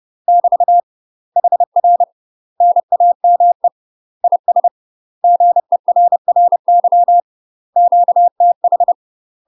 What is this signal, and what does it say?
Morse code, heard as a keyed tone, about 30 wpm, 700 Hz. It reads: <BT> HR NAME IS GERRY QTH
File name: UK MASTER QSOs TRAINING / 30WPM / UK MASTER QSOs 0009